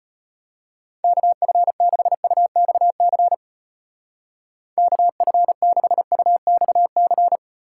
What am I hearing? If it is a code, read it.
Morse code, heard as a keyed tone, about 33 wpm, 700 Hz. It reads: KF6UXC KF6UXC